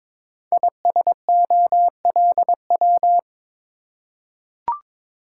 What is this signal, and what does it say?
Morse code, heard as a keyed tone, about 22 wpm, 700 Hz. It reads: ISOLW
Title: Code Groups - 5 Letter:Numbers - 22wpm